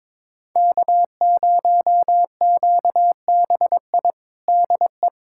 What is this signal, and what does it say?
Morse code, heard as a keyed tone, about 21 wpm, 700 Hz. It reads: K0QBI DE
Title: USA Callsign DE Callsign - 5 Characters - 22wpm